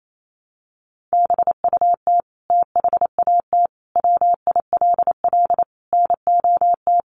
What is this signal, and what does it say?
Morse code, heard as a keyed tone, about 28 wpm, 700 Hz. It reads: BUT THAT WILL NOT
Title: Sentences from Top 900 Words 28wpm